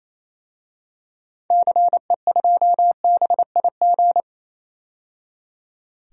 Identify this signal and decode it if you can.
Morse code, heard as a keyed tone, about 28 wpm, 700 Hz. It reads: CE2BIG
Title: DXCC-100 Call Signs - 6 Characters - 28wpm